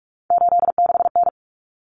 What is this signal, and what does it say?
Morse code, heard as a keyed tone, about 44 wpm, 700 Hz. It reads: 86N
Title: Common Cut Numbers - 3-Digit 45wpm